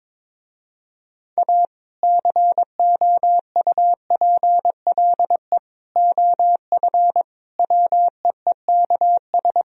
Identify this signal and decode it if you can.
Morse code, heard as a keyed tone, about 22 wpm, 700 Hz. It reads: A COUPLE OF WEEKS
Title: Speed-Racing - Sets of 5 Words 22wpm